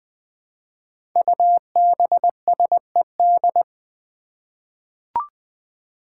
Morse code, heard as a keyed tone, about 20 wpm, 700 Hz. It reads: UBSED
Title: Five Challenging Letter-Number Combinations 20wpm